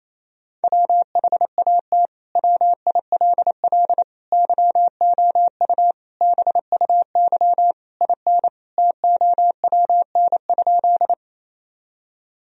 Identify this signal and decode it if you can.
Morse code, heard as a keyed tone, about 28 wpm, 700 Hz. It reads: WHAT WILL YOU BUY IN TOWN?
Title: Sentences from Top 1000 Words - Review Part 4 - 28wpm